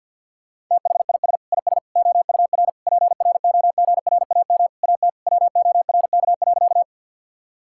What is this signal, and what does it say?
Morse code, heard as a keyed tone, about 50 wpm, 700 Hz. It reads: THIS IS OUR PROGRAM AT WORK.
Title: Sentences from Top 1000 Words - Review Part 5 - 50wpm